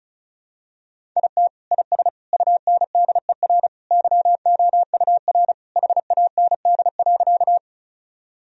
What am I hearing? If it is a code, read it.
Morse code, heard as a keyed tone, about 36 wpm, 700 Hz. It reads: IT IS UNDER YOUR HAND.
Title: Sentences from Top 1000 Words - Review Part 1 - 35wpm